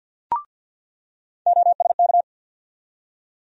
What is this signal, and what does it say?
Morse code, heard as a keyed tone, about 49 wpm, 700 Hz. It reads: OSK
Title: USA Call Sign Postfixes - 3 Letters 50wpm